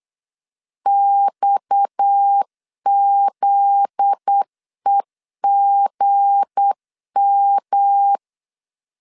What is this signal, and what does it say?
Morse code, heard as a keyed tone, about 8 wpm, 800 Hz. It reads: XZEGM
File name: LEZIONI / 8 YJ45 / 8 YJ 8wpm 3313